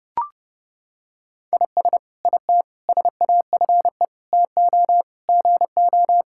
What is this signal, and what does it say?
Morse code, heard as a keyed tone, about 30 wpm, 700 Hz. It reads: IS IT SAFE TO GO
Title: Sentences from Top 1000 Words - Review Part 6 - 30wpm